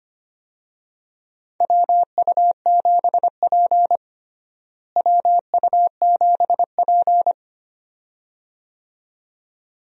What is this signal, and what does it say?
Morse code, heard as a keyed tone, about 25 wpm, 700 Hz. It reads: WU7P WU7P
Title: USA Call Signs - 4 Characters - Repeated 2x - 25wpm